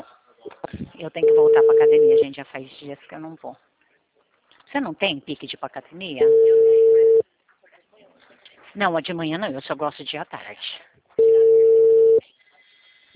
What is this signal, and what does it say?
Ringback tone (China, 450 Hz), with speech.
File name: out- -2008- - - .1851